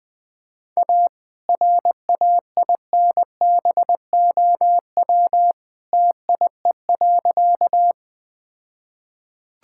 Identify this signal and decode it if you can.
Morse code, heard as a keyed tone, about 20 wpm, 700 Hz. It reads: A RAINBOW TIE.
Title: All-Age Appropriate Jokes - 20wpm